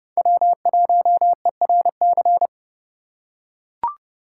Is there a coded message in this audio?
Yes: Morse code from 0.2 to 2.5 s, heard as a keyed tone, about 28 wpm, 700 Hz. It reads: W1ERC